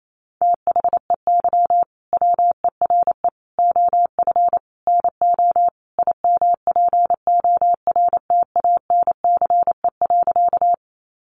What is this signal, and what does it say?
Morse code, heard as a keyed tone, about 28 wpm, 700 Hz. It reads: THEY WERE OF NO IMPORTANCE.
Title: Sentences from Top 1200 Words 28wpm